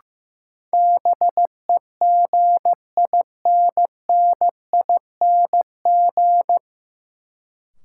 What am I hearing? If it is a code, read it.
Morse code, heard as a keyed tone, about 15 wpm, 700 Hz. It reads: BEGINNING
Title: Sentences from Top 800 Words - New Word Drill - 15wpm